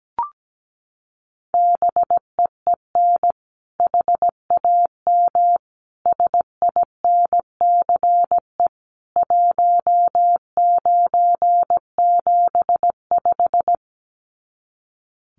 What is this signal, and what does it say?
Morse code, heard as a keyed tone, about 17 wpm, 700 Hz. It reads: BEEN HAM SINCE 1975
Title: Common QSO Phrases 17wpm